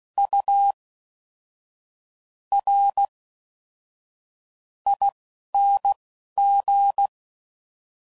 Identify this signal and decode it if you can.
Morse code, heard as a keyed tone, about 16 wpm, spaced as for 6 wpm, 800 Hz. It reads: URING